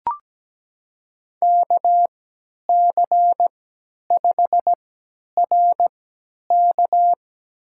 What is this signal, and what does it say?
Morse code, heard as a keyed tone, about 17 wpm, spaced as for 10 wpm, 700 Hz. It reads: KC5RK
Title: USA Call Signs - 5 Characters - 3x Character Spacing - 17wpm